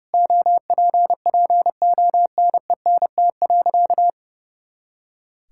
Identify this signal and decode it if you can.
Morse code, heard as a keyed tone, about 32 wpm, 700 Hz. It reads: OPPONENT.
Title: Pangram Challenge v2 - 30wpm